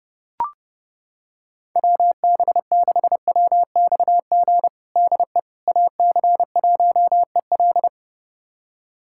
Morse code, heard as a keyed tone, about 30 wpm, 700 Hz. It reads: WB6WXG DE AC1EL
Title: USA Callsign DE Callsign 30wpm